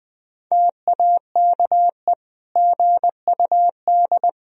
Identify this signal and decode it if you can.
Morse code, heard as a keyed tone, about 20 wpm, 700 Hz. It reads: TAKE GUD
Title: Sentences from Top 500 Words and Abbreviations 20wpm